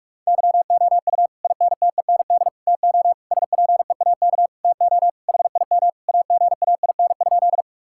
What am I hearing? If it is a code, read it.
Morse code, heard as a keyed tone, about 46 wpm, 700 Hz. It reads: YOU INTEND TO SPEAK TO HIM AGAIN?
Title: Sentences from Top 1600 Words 45wpm